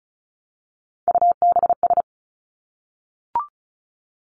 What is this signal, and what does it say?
Morse code, heard as a keyed tone, about 36 wpm, 700 Hz. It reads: UBS
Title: Three Challenging Letter-Number Combinations 35wpm